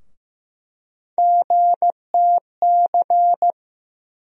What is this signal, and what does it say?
Morse code, heard as a keyed tone, about 15 wpm, 700 Hz. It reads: GTC
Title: USA Call Sign Postfixes - 3 Letters 15wpm